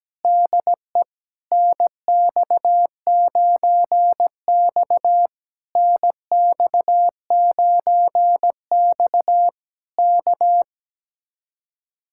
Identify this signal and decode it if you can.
Morse code, heard as a keyed tone, about 17 wpm, 700 Hz. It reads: DE NX9X NX9X K
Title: Speed-Racing Common QSO Phrases - 17wpm